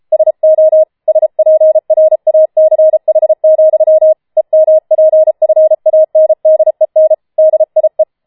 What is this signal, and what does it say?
Morse code, heard as a keyed tone, about 33 wpm, 630 Hz. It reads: SO SPRACH, EMPFANDEN DIE